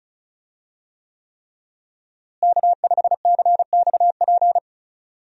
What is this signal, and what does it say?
Morse code, heard as a keyed tone, about 36 wpm, 700 Hz. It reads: K5CXP
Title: USA Call Signs - 5 Characters - Repeated 3x - 35wpm